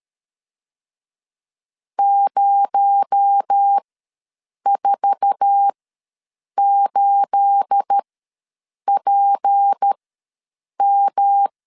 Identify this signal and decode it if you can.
Morse code, heard as a keyed tone, about 13 wpm, spaced as for 7 wpm, 800 Hz. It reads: 048PM
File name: LEZIONI / 9 28 ws 12wpm 3354